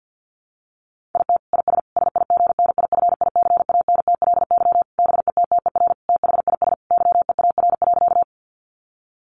Morse code, heard as a keyed tone, about 50 wpm, 700 Hz. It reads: IT IS SIGNIFICANTLY BETTER THIS YEAR.
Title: Sentences from Top 1500 Words 50wpm